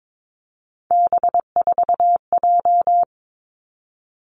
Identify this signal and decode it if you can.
Morse code, heard as a keyed tone, about 22 wpm, 700 Hz. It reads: B4J
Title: Three Challenging Letter-Number Combinations 22wpm